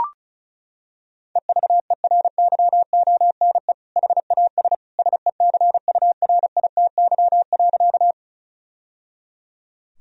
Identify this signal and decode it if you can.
Morse code, heard as a keyed tone, about 34 wpm, 700 Hz. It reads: EVERYONE HAS SECURITY.
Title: Sentences from Top 1000 Words - Review Part 4 - 35wpm